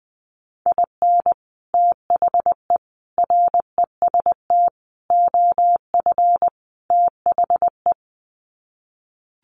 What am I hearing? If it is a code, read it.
Morse code, heard as a keyed tone, about 20 wpm, 700 Hz. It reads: IN THE REST OF THE
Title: Sets of 5 Words 20wpm